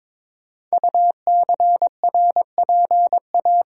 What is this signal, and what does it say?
Morse code, heard as a keyed tone, about 22 wpm, 700 Hz. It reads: UCRPA
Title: Code Groups - 5 Letter:Numbers - 22wpm